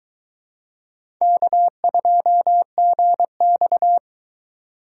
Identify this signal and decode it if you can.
Morse code, heard as a keyed tone, about 21 wpm, 700 Hz. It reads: K2GX